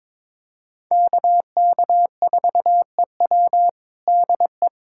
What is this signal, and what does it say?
Morse code, heard as a keyed tone, about 22 wpm, 700 Hz. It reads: KK4EW DE
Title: USA Callsign DE Callsign - 5 Characters - 22wpm